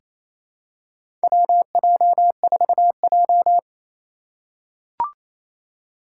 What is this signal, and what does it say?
Morse code, heard as a keyed tone, about 28 wpm, 700 Hz. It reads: WJ4J
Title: USA Call Signs - 2x Character Spacing - 28wpm